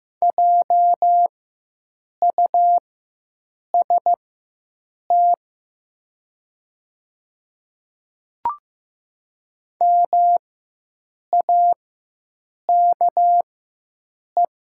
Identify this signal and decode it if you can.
Morse code, heard as a keyed tone, about 15 wpm, spaced as for 7 wpm, 700 Hz. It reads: JUST MAKE
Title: 4x Character Spacing - Top 300 Words - 15wpm